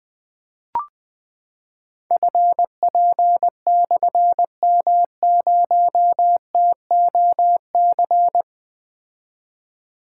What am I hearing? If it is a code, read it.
Morse code, heard as a keyed tone, about 20 wpm, 700 Hz. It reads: FP/M0TOC
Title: Difficult Call Signs 20wpm